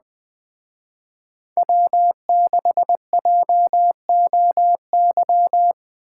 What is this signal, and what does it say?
Morse code, heard as a keyed tone, about 20 wpm, 700 Hz. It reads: W6JOY